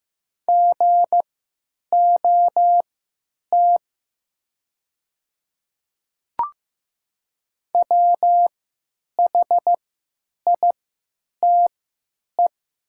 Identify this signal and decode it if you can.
Morse code, heard as a keyed tone, about 15 wpm, spaced as for 9 wpm, 700 Hz. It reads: GOT WHITE
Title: 3x Character Spacing - Top 300 Words - 15wpm